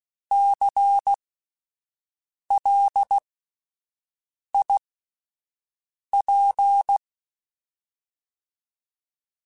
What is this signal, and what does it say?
Morse code, heard as a keyed tone, about 16 wpm, spaced as for 5 wpm, 800 Hz. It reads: CLIP
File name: Alphabetical / Alphabetical mp3s / Cs @5